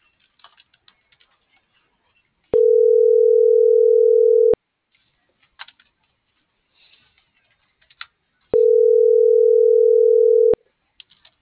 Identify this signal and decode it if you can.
Ringback tone (North America), 440 + 480 Hz.